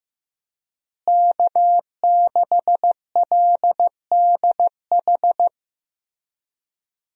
Morse code, heard as a keyed tone, about 15 wpm, 700 Hz. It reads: K6LDH
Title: USA Call Signs - 5 Characters - Repeated 2x - 15wpm